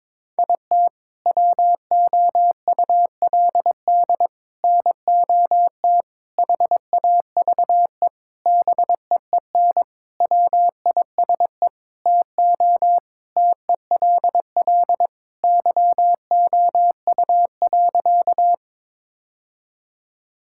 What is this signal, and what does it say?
Morse code, heard as a keyed tone, about 22 wpm, 700 Hz. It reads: IT WOULD NOT HAVE BEEN WISE TO TELL YOU.
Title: Sentences from Top 1600 Words 22wpm